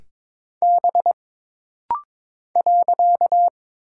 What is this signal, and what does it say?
Morse code, heard as a keyed tone, about 22 wpm, 700 Hz. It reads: B .
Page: This rapid-fire timing is geared to push your ICR (Instant Character Recognition) skills.